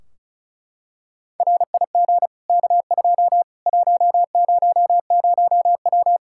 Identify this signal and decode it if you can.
Morse code, heard as a keyed tone, about 36 wpm, 700 Hz. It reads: RIG K2 100W
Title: K1USN SST QSO Practice 35wpm